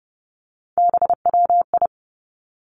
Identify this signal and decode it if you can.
Morse code, heard as a keyed tone, about 30 wpm, 700 Hz. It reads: BWI
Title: Three Challenging Letter-Number Combinations 30wpm